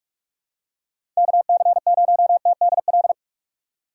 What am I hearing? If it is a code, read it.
Morse code, heard as a keyed tone, about 46 wpm, 700 Hz. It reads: KC0TDL
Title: USA Call Signs - 6 Characters - 6x Character Spacing - 45wpm